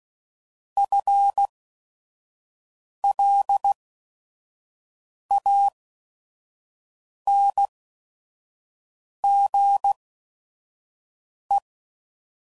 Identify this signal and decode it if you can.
Morse code, heard as a keyed tone, about 16 wpm, spaced as for 5 wpm, 800 Hz. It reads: FLANGE